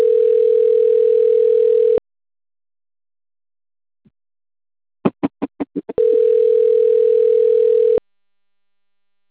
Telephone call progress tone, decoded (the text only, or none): ringback tone